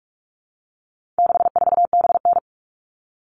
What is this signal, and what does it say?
Morse code, heard as a keyed tone, about 44 wpm, 700 Hz. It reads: 64BN
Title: All Cut Numbers - 4-Digit 45wpm